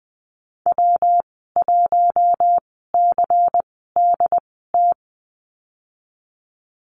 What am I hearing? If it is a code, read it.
Morse code, heard as a keyed tone, about 20 wpm, spaced as for 14 wpm, 700 Hz. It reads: W1CDT
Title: 161 Lesson 26g - Letter g - Call Signs 20wpm